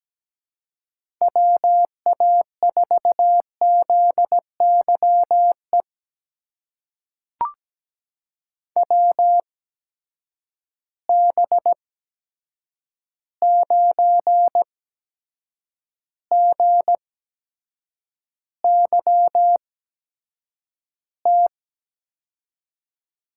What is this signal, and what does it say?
Morse code, heard as a keyed tone, about 17 wpm, spaced as for 7 wpm, 700 Hz. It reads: WA4ZYE WB9GYT
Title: USA Call Signs - 6 Characters - 8x Character Spacing - 17wpm